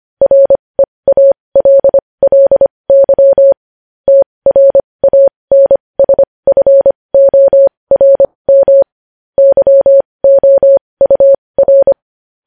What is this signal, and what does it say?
Morse code, heard as a keyed tone, about 25 wpm, spaced as for 20 wpm, 550 Hz. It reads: REALLY TRANSFORM YOUR
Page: news-headlines-in-morse-code-at-20-wpm